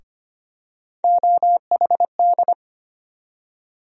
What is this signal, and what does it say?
Morse code, heard as a keyed tone, about 25 wpm, 700 Hz. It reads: OHD
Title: USA Call Sign Postfixes - 3 Letters 25wpm